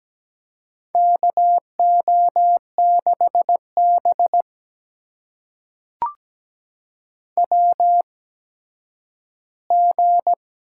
Morse code, heard as a keyed tone, about 17 wpm, spaced as for 10 wpm, 700 Hz. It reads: KO6B WG
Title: USA Call Signs - 4 Characters - 8x Character Spacing - 17wpm